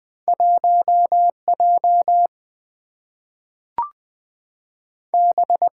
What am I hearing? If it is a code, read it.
Morse code, heard as a keyed tone, about 20 wpm, 700 Hz. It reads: 1 J B
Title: Two Letter-Number Combinations - 20wpm